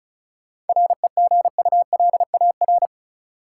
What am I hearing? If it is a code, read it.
Morse code, heard as a keyed tone, about 36 wpm, 700 Hz. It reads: REGULAR
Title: Sentences from Top 900 Words - New Word Drill - 35wpm